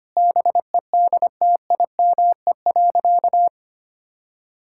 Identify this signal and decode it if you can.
Morse code, heard as a keyed tone, about 25 wpm, 700 Hz. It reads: BEDTIME.
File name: Easy Sentences with Hints - 25wpm